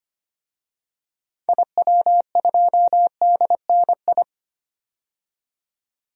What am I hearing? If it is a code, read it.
Morse code, heard as a keyed tone, about 25 wpm, 700 Hz. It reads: IW2DNI
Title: DXCC-100 Call Signs - 6 Characters - 25wpm